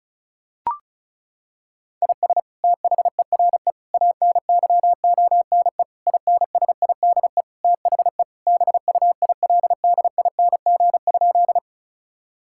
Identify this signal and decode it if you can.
Morse code, heard as a keyed tone, about 36 wpm, 700 Hz. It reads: IS THERE ANYONE INSIDE THE BUILDING?
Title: Sentences from Top 1000 Words - Review Part 6 - 35wpm